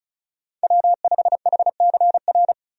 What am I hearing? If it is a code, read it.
Morse code, heard as a keyed tone, about 36 wpm, 700 Hz. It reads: W5HCR
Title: USA Call Signs - 5 Characters - 35wpm